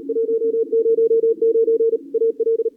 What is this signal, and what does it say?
Morse code, heard as a keyed tone, about 34 wpm, 440 Hz. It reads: 109 AL
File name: CWSprint / Feb2022 / k5tr-feb-na-sprint-cw-k5ot -0000